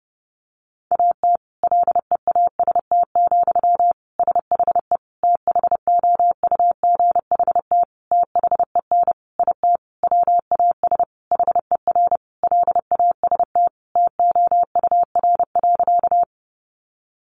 Morse code, heard as a keyed tone, about 30 wpm, 700 Hz. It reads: AT LEAST, SHE THOUGHT THEN IT WAS HER LAST TOUR.
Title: Sentences from Top 1400 Words 30wpm